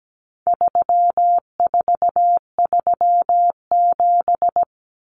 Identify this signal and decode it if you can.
Morse code, heard as a keyed tone, about 17 wpm, 700 Hz. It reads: 3437
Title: Common Cut Numbers - 4-Digit 17wpm